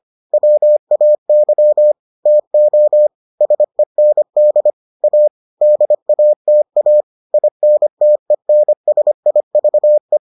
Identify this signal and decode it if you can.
Morse code, heard as a keyed tone, about 25 wpm, 600 Hz. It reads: WAY TO SEND A DATA INTENSIVE